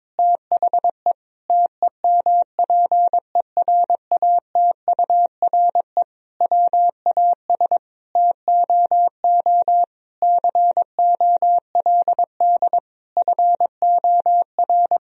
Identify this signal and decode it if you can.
Morse code, heard as a keyed tone, about 22 wpm, 700 Hz. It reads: THE TEMPERATURE WAS TOO COLD FOR
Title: Sentences from Top 1000 Words - Review Part 4 - 22wpm